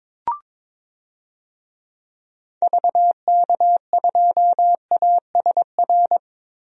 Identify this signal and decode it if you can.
Morse code, heard as a keyed tone, about 22 wpm, 700 Hz. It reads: VK2ASR
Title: DXCC-100 Call Signs - 6 Characters - Repeated 2x - 22wpm